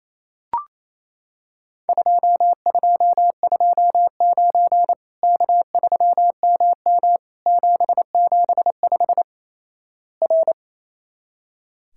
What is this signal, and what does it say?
Morse code, heard as a keyed tone, about 28 wpm, 700 Hz. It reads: 2229 K3MM 775 R
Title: Worked All Europe DX Contest - QTC Exchange - 28wpm